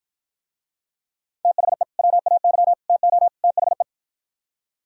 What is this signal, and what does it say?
Morse code, heard as a keyed tone, about 52 wpm, 700 Hz. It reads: THE WAY TO THE